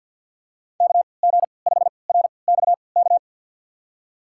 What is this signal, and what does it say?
Morse code, heard as a keyed tone, about 52 wpm, spaced as for 29 wpm, 690 Hz. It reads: KG5RXK
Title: USA Call Signs - 3x Character Spacing - 50wpm